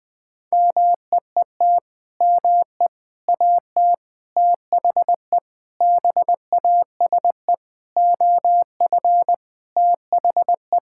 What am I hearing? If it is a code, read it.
Morse code, heard as a keyed tone, about 21 wpm, 700 Hz. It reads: MEET ME AT THE BASE OF THE
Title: Sentences with 5-Word Phrases - 20wpm